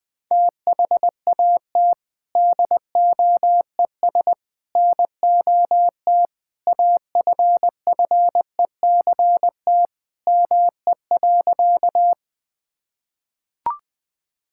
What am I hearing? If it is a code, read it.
Morse code, heard as a keyed tone, about 20 wpm, 700 Hz. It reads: THAT DOES NOT AFFECT ME.
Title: Sentences from Top 1000 Words - Review Part 6 - 20wpm